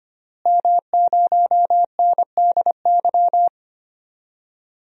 Morse code, heard as a keyed tone, about 25 wpm, 700 Hz. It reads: M0NDY
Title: DXCC-100 Call Signs - 5 Characters - Repeated 3x - 25wpm